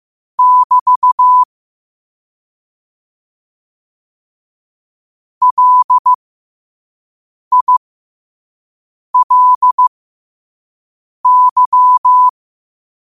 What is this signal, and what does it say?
Morse code, heard as a keyed tone, about 15 wpm, spaced as for 5 wpm, 1000 Hz. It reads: <BT> LILY
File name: qotdfiles / 5wpm